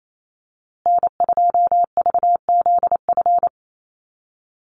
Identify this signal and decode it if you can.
Morse code, heard as a keyed tone, about 28 wpm, 700 Hz. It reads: N2VZF
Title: USA Call Signs Repeated 3x 28wpm